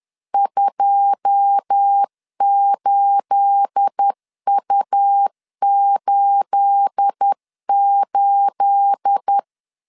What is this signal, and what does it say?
Morse code, heard as a keyed tone, about 10 wpm, 800 Hz. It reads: 28U88
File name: LEZIONI / 9 28 10wpm 2842